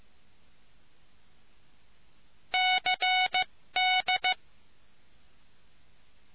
Morse code, heard as a keyed tone, about 15 wpm, spaced as for 13 wpm, 740 Hz. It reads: CD